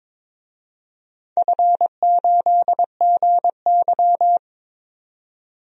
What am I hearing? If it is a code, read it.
Morse code, heard as a keyed tone, about 21 wpm, 700 Hz. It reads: F8GY
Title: DXCC-100 Call Signs - 4 Characters - 22wpm